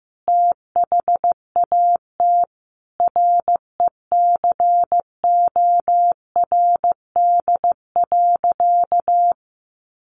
Morse code, heard as a keyed tone, about 15 wpm, 700 Hz. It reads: THAT RECORD.
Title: Sentences from Top 1400 Words 15wpm